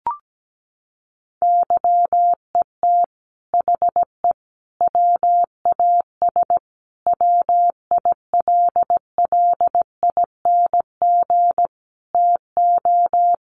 Morse code, heard as a keyed tone, about 17 wpm, 700 Hz. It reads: YET HE WAS WILLING TO
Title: Sentences from Top 1100 Words 17wpm